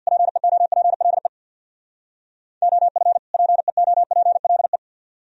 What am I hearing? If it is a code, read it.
Morse code, heard as a keyed tone, about 62 wpm, 700 Hz. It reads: PEOPLE OF PEOPLE
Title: Speed-Racing - Sets of 2 Words 50wpm